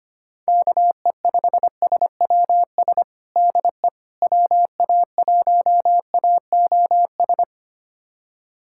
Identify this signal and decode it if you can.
Morse code, heard as a keyed tone, about 25 wpm, 700 Hz. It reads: KE5SWS DE WA1AOS
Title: USA Callsign DE Callsign - 6 Characters - 25wpm